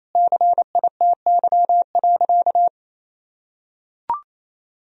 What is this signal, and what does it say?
Morse code, heard as a keyed tone, about 28 wpm, 700 Hz. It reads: CITY.
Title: Sentences from Top 1000 Words - Review Part 4 - 28wpm